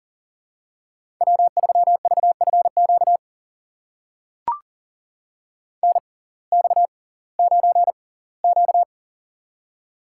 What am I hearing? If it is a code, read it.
Morse code, heard as a keyed tone, about 40 wpm, spaced as for 22 wpm, 700 Hz. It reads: W3VFQ NX9Q
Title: USA Call Signs - 6x Character Spacing - 40wpm